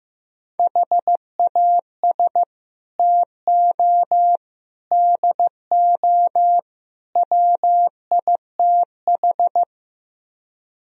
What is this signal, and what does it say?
Morse code, heard as a keyed tone, about 15 wpm, 700 Hz. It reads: HAS TO DO WITH
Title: Speed-Racing - Sets of 4 Words 15wpm